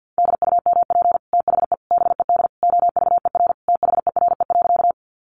Morse code, heard as a keyed tone, about 50 wpm, 700 Hz. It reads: DUMP THE BED OVER THERE.
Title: Sentences from Top 1300 Words 50wpm